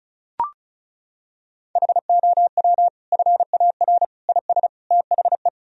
Morse code, heard as a keyed tone, about 34 wpm, 700 Hz. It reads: HOW FAR IS THE
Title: Sentences from Top 1000 Words - Review Part 4 - 35wpm